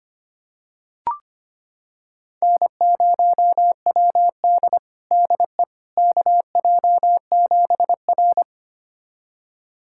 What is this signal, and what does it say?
Morse code, heard as a keyed tone, about 25 wpm, 700 Hz. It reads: N0WD DE KJ7R
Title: USA Callsign DE Callsign - 4 Characters - 25wpm